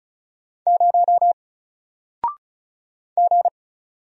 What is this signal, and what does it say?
Morse code, heard as a keyed tone, about 36 wpm, 700 Hz. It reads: 0 G
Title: Single Character (Letter:Number:Punctuation:Prosign) 35wpm